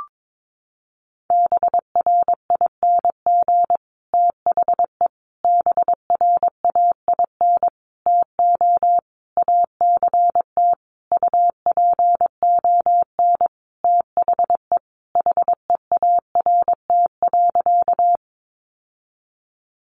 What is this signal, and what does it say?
Morse code, heard as a keyed tone, about 22 wpm, 700 Hz. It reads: BRING THE BRAIN TO ACT UPON THE HEART.
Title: Sentences from Top 1200 Words 22wpm